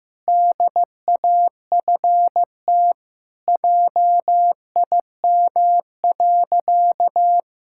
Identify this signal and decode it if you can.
Morse code, heard as a keyed tone, about 15 wpm, 700 Hz. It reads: DAFT JIM.
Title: Pangram Challenge v2 - 15wpm